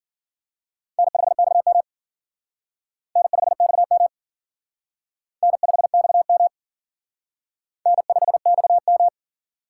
Morse code, heard as a keyed tone, about 50 wpm, 700 Hz. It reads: N5XM N5XM N5XM N5XM
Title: Speed Racing - USA 1x2 Call Signs 40wpm